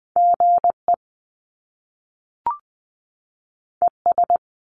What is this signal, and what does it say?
Morse code, heard as a keyed tone, about 20 wpm, 700 Hz. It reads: GE ES
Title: Top Two Letter Combinations in Words 20wpm